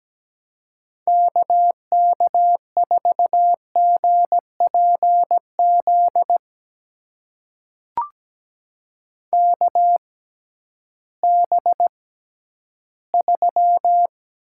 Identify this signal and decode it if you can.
Morse code, heard as a keyed tone, about 17 wpm, spaced as for 11 wpm, 700 Hz. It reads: KK4GPZ KB3
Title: USA Call Signs - 6x Character Spacing - 17wpm